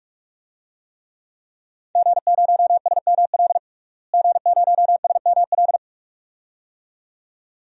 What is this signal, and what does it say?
Morse code, heard as a keyed tone, about 41 wpm, 700 Hz. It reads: G0SML G0SML
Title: DXCC-100 Call Signs - 5 Characters - Repeated 2x - 45wpm